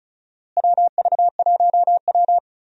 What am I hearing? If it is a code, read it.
Morse code, heard as a keyed tone, about 36 wpm, 700 Hz. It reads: WV1W
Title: USA Call Signs - 4 Characters - Repeated 3x - 35wpm